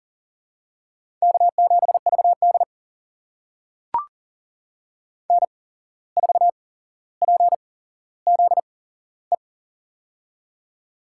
Morse code, heard as a keyed tone, about 42 wpm, spaced as for 16 wpm, 700 Hz. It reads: K7VD N4PZE
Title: USA Call Signs - 8x Character Spacing - 40wpm